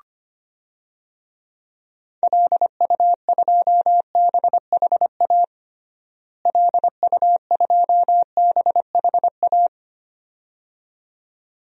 Morse code, heard as a keyed tone, about 25 wpm, 700 Hz. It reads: LU2BHA LU2BHA
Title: DXCC-100 Call Signs - 6 Characters - Repeated 2x - 25wpm